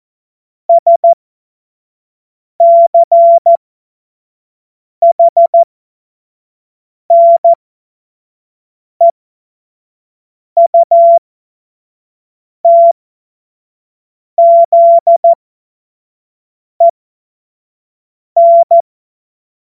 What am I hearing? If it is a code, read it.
Morse code, heard as a keyed tone, about 14 wpm, spaced as for 5 wpm, 680 Hz. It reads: SCHNEUTZEN
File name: Morsekurs / Witze / Witz 09. Dezember 2025 25 F70